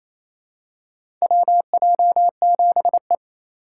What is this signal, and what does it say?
Morse code, heard as a keyed tone, about 27 wpm, 700 Hz. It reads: WJ7E
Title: USA Call Signs - 4 Characters - 4x Character Spacing - 28wpm